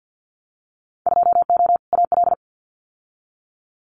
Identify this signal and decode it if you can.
Morse code, heard as a keyed tone, about 50 wpm, 700 Hz. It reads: 2O AL
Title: Winter Field Day Exchanges 50wpm